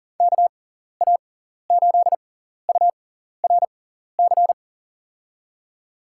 Morse code, heard as a keyed tone, about 40 wpm, spaced as for 14 wpm, 700 Hz. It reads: KA8URC
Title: USA Call Signs - 6 Characters - 6x Character Spacing - 40wpm